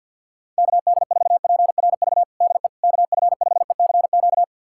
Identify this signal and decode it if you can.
Morse code, heard as a keyed tone, about 52 wpm, 700 Hz. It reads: KD4PRV DE KF5ECQ
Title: USA Callsign DE Callsign - 6 Characters - 50wpm